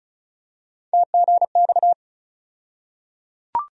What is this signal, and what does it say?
Morse code, heard as a keyed tone, about 34 wpm, 700 Hz. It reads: TGX
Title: USA Call Sign Postfixes - 3 Letters 35wpm